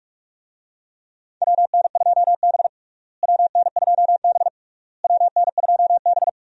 Morse code, heard as a keyed tone, about 41 wpm, 700 Hz. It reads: WN2B WN2B WN2B
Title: USA Call Signs - 4 Characters - Repeated 3x - 45wpm